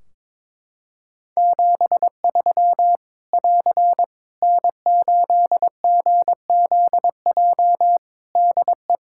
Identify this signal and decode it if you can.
Morse code, heard as a keyed tone, about 22 wpm, 700 Hz. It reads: 73 <AR> N8GZJ DE
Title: Practice QSO - Exchange 4 - Ending - 22wpm